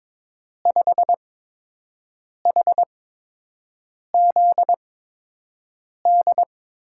Morse code, heard as a keyed tone, about 22 wpm, spaced as for 6 wpm, 700 Hz. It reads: 5HZD
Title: USA Call Signs - 6 Characters - 8x Character Spacing - 22wpm